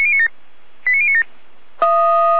Five-tone selective call (ZVEI), heard as quick runs of five tones, tones of about 70 ms. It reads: 89097
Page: /SDR/ZVEI/